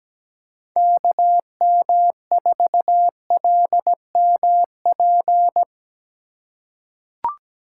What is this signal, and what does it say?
Morse code, heard as a keyed tone, about 17 wpm, 700 Hz. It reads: KM4LMP
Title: USA Call Signs - 6 Characters - 2x Character Spacing - 17wpm